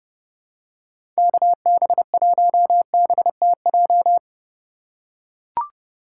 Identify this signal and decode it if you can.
Morse code, heard as a keyed tone, about 31 wpm, 700 Hz. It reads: KB1BTJ
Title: USA Call Signs - 6 Characters - 2x Character Spacing - 30wpm